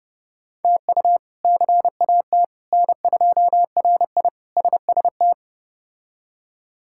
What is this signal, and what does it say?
Morse code, heard as a keyed tone, about 32 wpm, 700 Hz. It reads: TU CAT N2RI SST
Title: K1USN SST QSO Practice - 30wpm